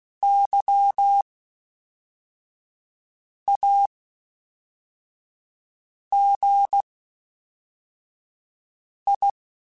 Morse code, heard as a keyed tone, about 16 wpm, spaced as for 4 wpm, 800 Hz. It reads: YAGI